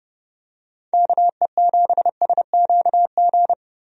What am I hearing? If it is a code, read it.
Morse code, heard as a keyed tone, about 30 wpm, 700 Hz. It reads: KE7SQG
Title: USA Call Signs - 6 Characters - 30wpm